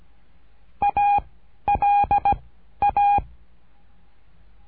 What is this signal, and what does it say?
Morse code, heard as a keyed tone, about 17 wpm, 810 Hz. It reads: A L A
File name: cursos / CURSO DE TELEGRAFIA / Telegrafia - Trilha 13